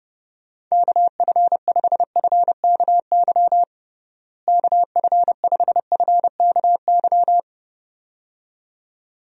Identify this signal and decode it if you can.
Morse code, heard as a keyed tone, about 30 wpm, 700 Hz. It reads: KF5FKY KF5FKY
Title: USA Call Signs - 6 Characters - Repeated 2x - 30wpm